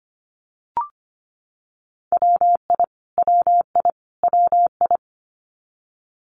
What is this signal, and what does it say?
Morse code, heard as a keyed tone, about 25 wpm, 700 Hz. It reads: WI WI WI
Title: US States Repeated 3x 25wpm